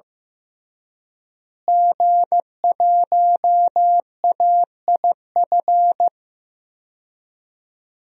Morse code, heard as a keyed tone, about 15 wpm, 700 Hz. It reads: G1AIF
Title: DXCC-100 Call Signs - 5 Characters - Repeated 3x - 15wpm